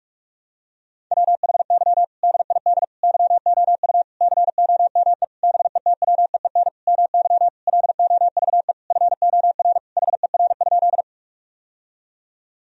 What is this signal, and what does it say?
Morse code, heard as a keyed tone, about 46 wpm, 700 Hz. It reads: WHY DID YOU COME BETWEEN MY LOVE FOR HER?
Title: Sentences from Top 1000 Words - Review Part 1 - 45wpm